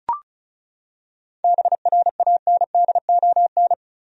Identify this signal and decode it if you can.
Morse code, heard as a keyed tone, about 34 wpm, 700 Hz. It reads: BRANDON
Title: USA Common Names - 35wpm